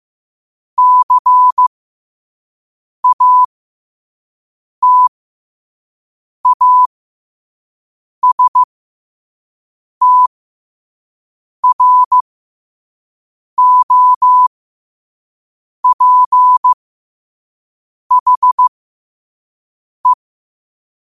Morse code, heard as a keyed tone, about 15 wpm, spaced as for 5 wpm, 1000 Hz. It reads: CATASTROPHE
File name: qotdfiles / 5wpm